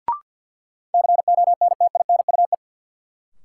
Morse code, heard as a keyed tone, about 52 wpm, 700 Hz. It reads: CONTINUE
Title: Sentences from Top 1100 Words - New Word Drill Alt - 50wpm